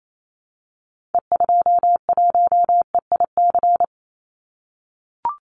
Morse code, heard as a keyed tone, about 28 wpm, 700 Hz. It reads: E21EIC
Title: Call Signs - 28wpm